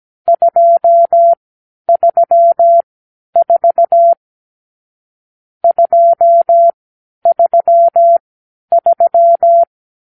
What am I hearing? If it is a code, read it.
Morse code, heard as a keyed tone, about 17 wpm, spaced as for 10 wpm, 680 Hz. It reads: 234 233
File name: k7qo / Lesson 46 (1-4)